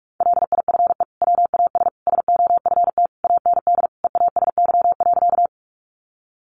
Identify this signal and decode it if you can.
Morse code, heard as a keyed tone, about 44 wpm, 700 Hz. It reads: LIFE WAS SOFT AND EASY.
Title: Sentences from Top 1500 Words 45wpm